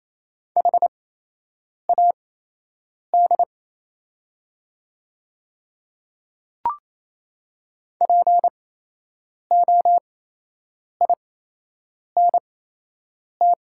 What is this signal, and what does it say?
Morse code, heard as a keyed tone, about 27 wpm, spaced as for 8 wpm, 690 Hz. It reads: HAD POINT
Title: 8x Character Spacing - Top 200 Words - 28wpm